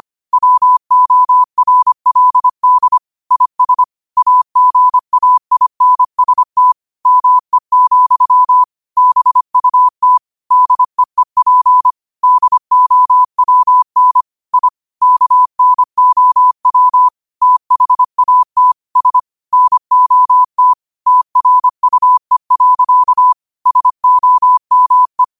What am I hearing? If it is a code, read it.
Morse code, heard as a keyed tone, about 25 wpm, 1000 Hz. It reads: WORLD IS AGAINST ME, BUT DEEP DOWN I KNOW THAT S NOT TRUE. SOME
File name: 25wpm